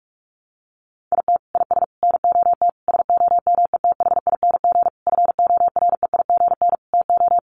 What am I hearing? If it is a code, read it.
Morse code, heard as a keyed tone, about 44 wpm, 700 Hz. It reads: IT IS NOT SOMETHING FOREIGN TO
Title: Sentences from Top 1400 Words 45wpm